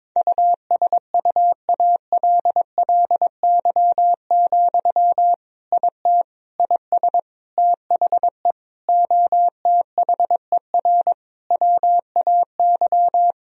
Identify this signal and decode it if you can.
Morse code, heard as a keyed tone, about 22 wpm, 700 Hz. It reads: USUALLY, IT IS THE OTHER WAY
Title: Sentences from Top 1000 Words - Review Part 3 - 22wpm